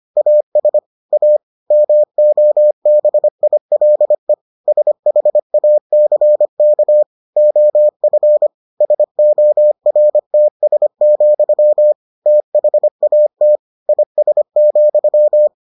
Morse code, heard as a keyed tone, about 25 wpm, 600 Hz. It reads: AS A MOBILE SHACK OF SORTS, THAT IS,